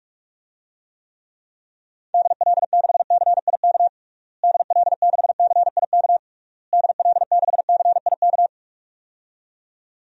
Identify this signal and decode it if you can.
Morse code, heard as a keyed tone, about 42 wpm, 700 Hz. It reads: DL6CIK DL6CIK DL6CIK
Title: DXCC-100 Call Signs - 6 Characters - Repeated 3x - 45wpm